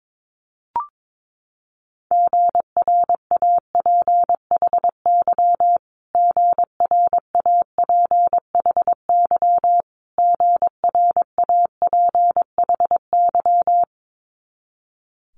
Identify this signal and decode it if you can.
Morse code, heard as a keyed tone, about 22 wpm, 700 Hz. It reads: GRAPHY GRAPHY GRAPHY
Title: Multi-letter Phonemes Repeated 3x 22wpm